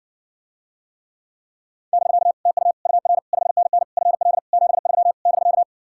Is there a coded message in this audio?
Yes: Morse code from 1.9 to 5.6 s, heard as a keyed tone, about 69 wpm, 700 Hz. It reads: <BK> TU UR 5NN FL 73 <BK>